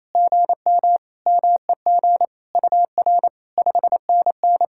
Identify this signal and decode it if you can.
Morse code, heard as a keyed tone, about 28 wpm, 700 Hz. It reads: GM MEG UR 5NN
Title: SOTA QSO Practice 28wpm